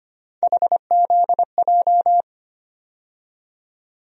Morse code, heard as a keyed tone, about 25 wpm, 700 Hz. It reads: HZJ
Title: USA Call Sign Postfixes - 3 Letters 25wpm